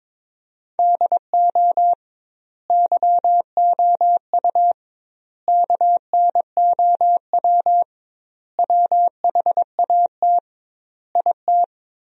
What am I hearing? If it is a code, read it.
Morse code, heard as a keyed tone, about 22 wpm, 700 Hz. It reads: DO YOU KNOW WHAT IT
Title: 2x Word-Spacing - Sentences from Top 100 Words - 22wpm